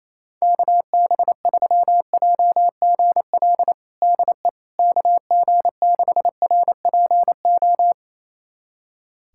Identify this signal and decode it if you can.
Morse code, heard as a keyed tone, about 28 wpm, 700 Hz. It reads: KB3JGL DE KG6RPO
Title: USA Callsign DE Callsign 28wpm